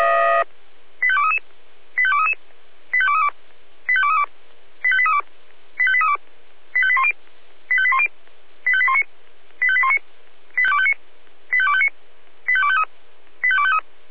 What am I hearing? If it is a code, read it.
Five-tone selective call (ZVEI), heard as quick runs of five tones, tones of about 70 ms. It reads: 85233 85233 85232 85232 85832 85832 85810 85810 85819 85819 85259 85253 85253
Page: ZVEI